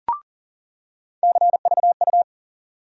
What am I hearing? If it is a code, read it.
Morse code, heard as a keyed tone, about 40 wpm, 700 Hz. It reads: CVU
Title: USA Call Sign Postfixes - 3 Letters 40wpm